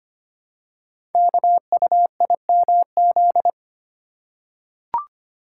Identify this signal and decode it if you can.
Morse code, heard as a keyed tone, about 25 wpm, 700 Hz. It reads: KUIMZ
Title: Code Groups - 5 Letters - 25wpm